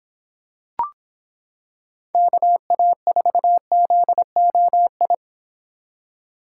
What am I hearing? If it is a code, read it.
Morse code, heard as a keyed tone, about 25 wpm, 700 Hz. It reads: KA4ZOI